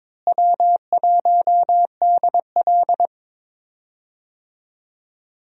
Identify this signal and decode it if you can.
Morse code, heard as a keyed tone, about 21 wpm, 700 Hz. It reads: W1DL
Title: USA Call Signs - 4 Characters - Repeated 2x - 22wpm